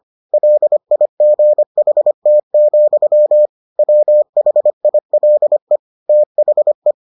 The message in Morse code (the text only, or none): LIGHT, WHILE THE